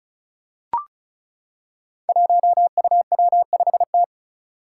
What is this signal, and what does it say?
Morse code, heard as a keyed tone, about 36 wpm, 700 Hz. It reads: 1UW5T
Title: Five Challenging Letter-Number Combinations 35wpm